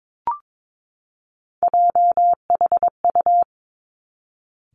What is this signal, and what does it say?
Morse code, heard as a keyed tone, about 22 wpm, 700 Hz. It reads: JHU
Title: Three Challenging Letter-Number Combinations 22wpm